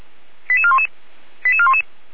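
Five-tone selective call (ZVEI), heard as quick runs of five tones, tones of about 70 ms. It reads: 88411 88411
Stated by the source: ZVEI five-tone selective call received on SDR.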